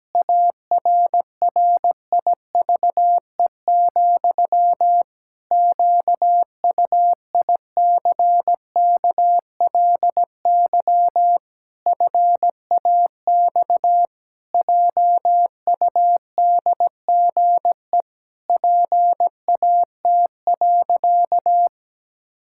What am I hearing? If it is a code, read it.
Morse code, heard as a keyed tone, about 17 wpm, 700 Hz. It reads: ARRIVE, QUICKLY FAX JUDGE PAT.
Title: Pangram Challenge v2 - 17wpm